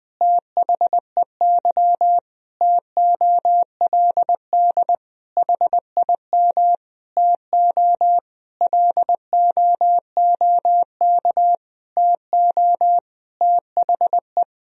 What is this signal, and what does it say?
Morse code, heard as a keyed tone, about 20 wpm, 700 Hz. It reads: THEY TOLD HIM TO LOOK TO THE
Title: Sentences from Top 1000 Words - Review Part 3 - 20wpm